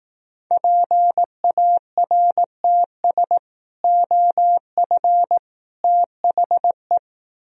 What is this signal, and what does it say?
Morse code, heard as a keyed tone, about 19 wpm, 700 Hz. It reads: PARTS OF THE
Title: Speed-Racing - Sets of 3 Words 15wpm